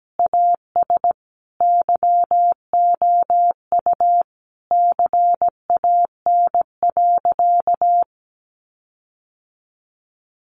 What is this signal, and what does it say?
Morse code, heard as a keyed tone, about 17 wpm, 700 Hz. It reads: AS YOU CAN.
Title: Sentences from Top 1400 Words 17wpm